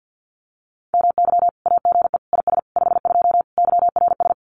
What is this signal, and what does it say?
Morse code, heard as a keyed tone, about 50 wpm, 700 Hz. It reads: MY AGE IS 52 YRS
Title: Common QSO Phrases Encore 50wpm